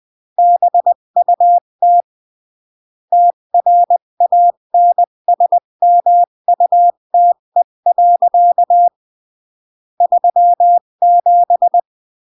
Morse code, heard as a keyed tone, about 20 wpm, 700 Hz. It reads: BUT TRANSMUTE. 37
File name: CW / jodorowsky / 82 commandments 20-15